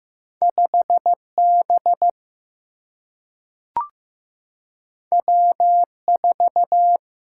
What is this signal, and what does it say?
Morse code, heard as a keyed tone, about 15 wpm, 700 Hz. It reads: 5 B W 4
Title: Single-Letter Single-Number 15wpm